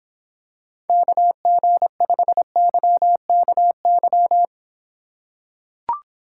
Morse code, heard as a keyed tone, about 26 wpm, 700 Hz. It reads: KG5YKY